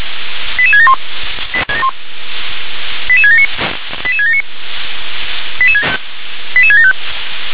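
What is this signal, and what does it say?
Five-tone selective call (ZVEI), heard as quick runs of five tones, tones of about 70 ms. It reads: 88671 88679 88679 88675
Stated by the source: ZVEI five-tone selective call received on SDR.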